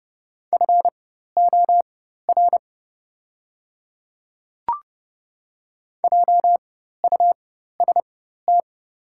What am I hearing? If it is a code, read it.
Morse code, heard as a keyed tone, about 30 wpm, spaced as for 14 wpm, 700 Hz. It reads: FOR JUST
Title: 4x Character Spacing - Top 200 Words - 30wpm